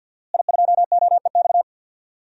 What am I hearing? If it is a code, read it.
Morse code, heard as a keyed tone, about 51 wpm, 700 Hz. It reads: I2OEX
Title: DXCC-100 Call Signs - 5 Characters - Repeated 2x - 50wpm